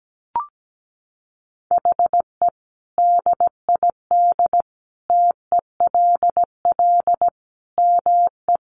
Morse code, heard as a keyed tone, about 17 wpm, 700 Hz. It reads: HE DID TELL ME
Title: Sentences from Top 1000 Words 17wpm